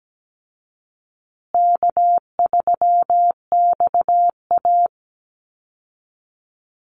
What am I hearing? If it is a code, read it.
Morse code, heard as a keyed tone, about 17 wpm, 700 Hz. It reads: K3XA
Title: Call Signs Repeated 2x - 17wpm